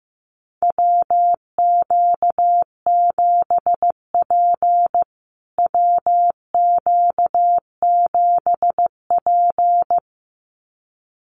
Morse code, heard as a keyed tone, about 15 wpm, 700 Hz. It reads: WQ7P WQ7P
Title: USA Call Signs Repeated 2x 15wpm